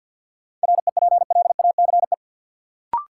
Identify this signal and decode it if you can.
Morse code, heard as a keyed tone, about 53 wpm, 700 Hz. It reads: REPLACE
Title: Sentences from Top 1400 Words - New Word Drill - 50wpm